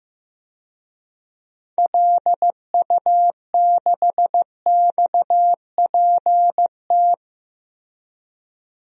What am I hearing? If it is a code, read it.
Morse code, heard as a keyed tone, about 15 wpm, 700 Hz. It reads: LU6XPT
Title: DXCC-100 Call Signs - 6 Characters - Repeated 2x - 15wpm